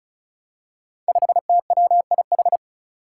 Morse code, heard as a keyed tone, about 36 wpm, 700 Hz. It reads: 5TWIH
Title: Five Challenging Letter-Number Combinations 35wpm